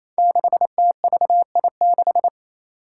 Morse code, heard as a keyed tone, about 28 wpm, 700 Hz. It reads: 6TVI6
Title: Five Challenging Letter-Number Combinations 28wpm